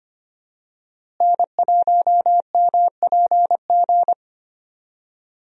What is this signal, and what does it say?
Morse code, heard as a keyed tone, about 25 wpm, 700 Hz. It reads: N1MPG
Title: USA Call Signs - 5 Characters - Repeated 2x - 25wpm